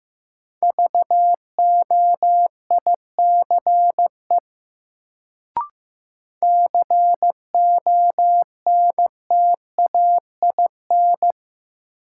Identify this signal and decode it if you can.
Morse code, heard as a keyed tone, about 15 wpm, 700 Hz. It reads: VOICE CONTAIN
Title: Sentences from Top 500 Words - New Word Drill - 15wpm